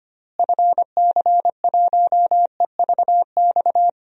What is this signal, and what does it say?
Morse code, heard as a keyed tone, about 25 wpm, 700 Hz. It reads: FC1EVX
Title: DXCC-100 Call Signs - 6 Characters - Repeated 3x - 25wpm